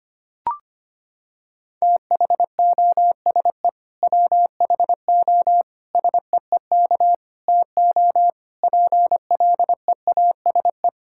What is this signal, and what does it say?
Morse code, heard as a keyed tone, about 25 wpm, 700 Hz. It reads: THOSE WHO SEEK TO PLEASE
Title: Sentences from Top 1600 Words 25wpm